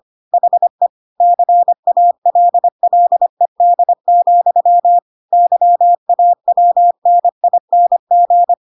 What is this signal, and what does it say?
Morse code, heard as a keyed tone, about 25 wpm, 700 Hz. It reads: HE CALLED, YAWNING